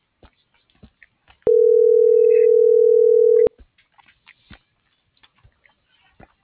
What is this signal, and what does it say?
Ringback tone (North America), 440 + 480 Hz.